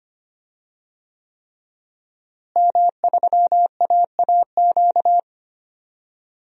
Morse code, heard as a keyed tone, about 25 wpm, 700 Hz. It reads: M3AAQ
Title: DXCC-100 Call Signs - 5 Characters - Repeated 3x - 25wpm